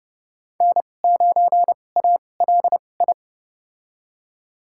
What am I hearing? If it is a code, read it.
Morse code, heard as a keyed tone, about 30 wpm, spaced as for 22 wpm, 700 Hz. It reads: N9ALI
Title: USA Call Signs - 5 Characters - 2x Character Spacing - 30wpm